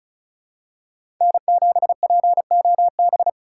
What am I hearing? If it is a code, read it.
Morse code, heard as a keyed tone, about 36 wpm, 700 Hz. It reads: N7POB
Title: USA Call Signs - 5 Characters - 4x Character Spacing - 35wpm